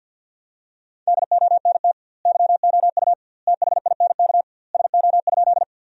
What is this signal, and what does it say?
Morse code, heard as a keyed tone, about 52 wpm, 700 Hz. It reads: DONT YOU THINK SO?
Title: 2x Word-Spacing - Sentences from Top 300 Words - 50wpm